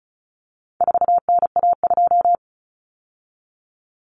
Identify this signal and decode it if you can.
Morse code, heard as a keyed tone, about 34 wpm, 700 Hz. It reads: 4NA2
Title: Common Cut Numbers - 4-Digit 35wpm